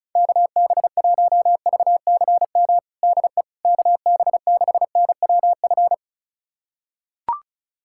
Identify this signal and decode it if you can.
Morse code, heard as a keyed tone, about 36 wpm, 700 Hz. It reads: KB1VCM DE KB6NWF
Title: USA Callsign DE Callsign - 6 Characters - 35wpm